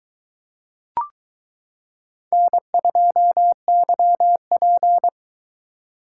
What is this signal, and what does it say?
Morse code, heard as a keyed tone, about 22 wpm, 700 Hz. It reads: N2YP